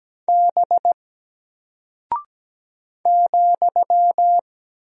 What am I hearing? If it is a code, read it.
Morse code, heard as a keyed tone, about 17 wpm, 700 Hz. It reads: B ,
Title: Single Character (Letter:Number:Punctuation:Prosign) 17wpm